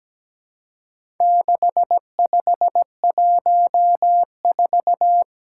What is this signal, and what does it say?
Morse code, heard as a keyed tone, about 17 wpm, 700 Hz. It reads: 6514
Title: Four Challenging Letter-Number Combinations 17wpm